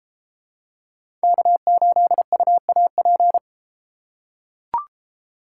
Morse code, heard as a keyed tone, about 31 wpm, 700 Hz. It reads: K8UAP